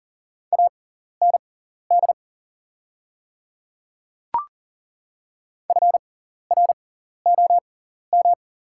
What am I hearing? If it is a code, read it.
Morse code, heard as a keyed tone, about 40 wpm, spaced as for 14 wpm, 700 Hz. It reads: AND FROM
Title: 6x Character Spacing - Top 300 Words - 40wpm